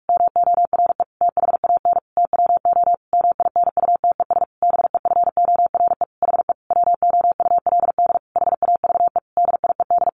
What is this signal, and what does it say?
Morse code, heard as a keyed tone, about 44 wpm, 700 Hz. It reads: MORE THAN TWO MINUTES BEFORE HE WOULD HAVE DIED
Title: A Sound Of Thunder 45wpm